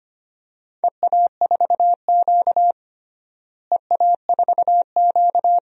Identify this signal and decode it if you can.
Morse code, heard as a keyed tone, about 25 wpm, 700 Hz. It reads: EA4Q EA4Q
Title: DXCC-100 Call Signs - 4 Characters - Repeated 2x - 25wpm